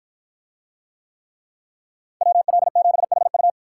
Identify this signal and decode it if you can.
Morse code, heard as a keyed tone, about 53 wpm, 700 Hz. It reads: WL7HU